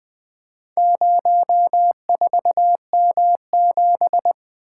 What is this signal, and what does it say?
Morse code, heard as a keyed tone, about 20 wpm, 700 Hz. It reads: 04M7
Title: Four Letter-Number Combinations - 20wpm